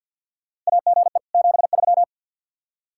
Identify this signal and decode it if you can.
Morse code, heard as a keyed tone, about 52 wpm, 700 Hz. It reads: AGE 73
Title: Practice QSO - Exchange 3 - More Info - 50wpm